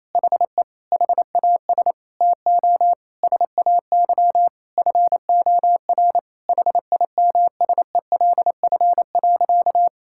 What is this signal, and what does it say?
Morse code, heard as a keyed tone, about 27 wpm, 700 Hz. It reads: HE HAS TO SAY FOR HIMSELF.
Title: Sentences from Top 1600 Words 28wpm